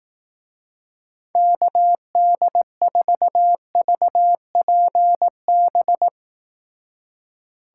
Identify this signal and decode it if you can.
Morse code, heard as a keyed tone, about 18 wpm, 700 Hz. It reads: KD4VPB
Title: Speed Racing - USA 2x3 Call Signs 15wpm